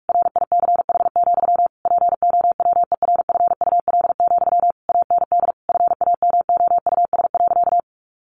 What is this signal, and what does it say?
Morse code, heard as a keyed tone, about 44 wpm, 700 Hz. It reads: RICH, POWERFUL, AND FAMOUS.
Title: Sentences from Top 1500 Words 45wpm